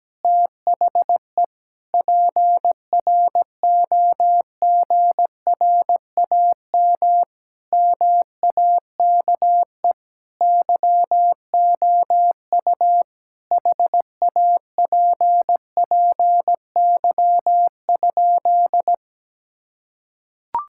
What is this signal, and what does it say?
Morse code, heard as a keyed tone, about 17 wpm, 700 Hz. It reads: THE PROGRAM MAKE YOU HAPPY?
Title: Sentences from Top 1000 Words - Review Part 5 - 17wpm